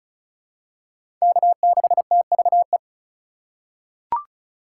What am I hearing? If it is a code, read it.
Morse code, heard as a keyed tone, about 36 wpm, 700 Hz. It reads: K6TVE
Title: USA Call Signs - 3x Character Spacing - 35wpm